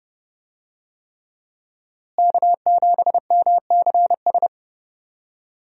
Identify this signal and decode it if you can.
Morse code, heard as a keyed tone, about 30 wpm, 700 Hz. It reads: K7MCS
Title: USA Call Signs - 5 Characters - Repeated 3x - 30wpm